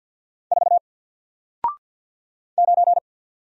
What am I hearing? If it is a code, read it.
Morse code, heard as a keyed tone, about 50 wpm, 700 Hz. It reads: 4 9
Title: Single Character (Letter:Number:Punctuation:Prosign) 50wpm